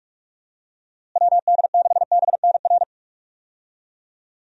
Morse code, heard as a keyed tone, about 41 wpm, 700 Hz. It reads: WD6BNR
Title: USA Call Signs - 6 Characters - Repeated 2x - 45wpm